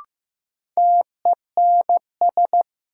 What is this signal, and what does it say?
Morse code, heard as a keyed tone, about 15 wpm, 700 Hz. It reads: TENS
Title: Sets of 4 Words - New Word Drill - 15wpm